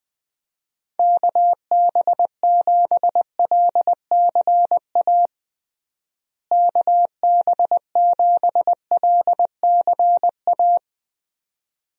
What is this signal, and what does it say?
Morse code, heard as a keyed tone, about 20 wpm, 700 Hz. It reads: KB7LCA KB7LCA
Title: USA Call Signs - 6 Characters - Repeated 2x - 20wpm